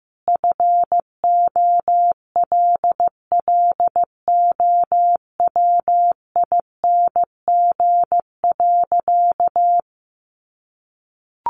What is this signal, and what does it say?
Morse code, heard as a keyed tone, about 15 wpm, 700 Hz. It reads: FOLLOWING.
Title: Sentences from Top 1300 Words 15wpm